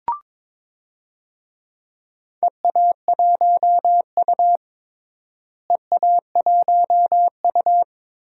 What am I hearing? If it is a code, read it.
Morse code, heard as a keyed tone, about 21 wpm, 700 Hz. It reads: EA1U EA1U
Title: DXCC-100 Call Signs - 4 Characters - Repeated 2x - 22wpm